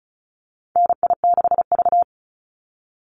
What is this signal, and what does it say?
Morse code, heard as a keyed tone, about 34 wpm, 700 Hz. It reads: NI6V
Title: Call Signs - 35wpm